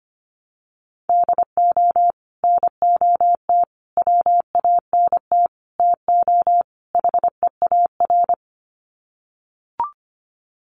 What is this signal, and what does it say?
Morse code, heard as a keyed tone, about 25 wpm, 700 Hz. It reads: DO NOT WANT TO HEAR
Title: Sets of 5 Words 25wpm